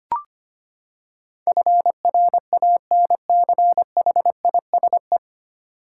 Morse code, heard as a keyed tone, about 25 wpm, 700 Hz. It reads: FRANCHISE
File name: Nine Letter Words - 25wpm